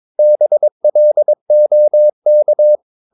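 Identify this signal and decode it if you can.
Morse code, heard as a keyed tone, about 22 wpm, 600 Hz. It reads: BLOK
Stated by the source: Morse practice of short Danish words at 22 wpm, from CW med Gnister.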